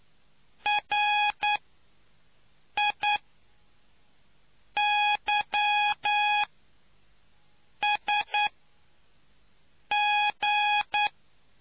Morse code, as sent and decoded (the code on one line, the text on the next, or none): .-. .. -.-- ... --.
RIYSG